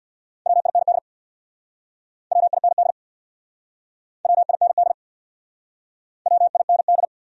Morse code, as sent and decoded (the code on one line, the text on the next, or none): .-- .. -. -.. / .-- .. -. -.. / .-- .. -. -.. / .-- .. -. -..
WIND WIND WIND WIND